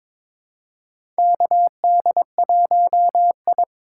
Morse code, heard as a keyed tone, about 21 wpm, 700 Hz. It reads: KD1I
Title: USA Call Signs - 4x Character Spacing - 22wpm